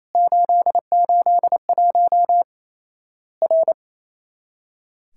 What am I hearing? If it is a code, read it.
Morse code, heard as a keyed tone, about 28 wpm, 700 Hz. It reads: 881 R
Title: Worked All Europe DX Contest - QTC Exchange - 28wpm